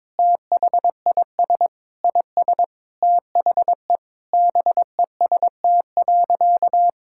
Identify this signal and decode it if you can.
Morse code, heard as a keyed tone, about 22 wpm, 700 Hz. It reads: THIS IS THE BEST.
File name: practice / 22wpm / Sentences with 3-Word Phrases - 22wpm